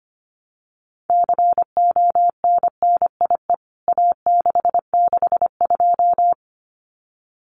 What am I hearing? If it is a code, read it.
Morse code, heard as a keyed tone, about 25 wpm, 700 Hz. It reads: CONNIE A662
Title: CWT Exchange 25wpm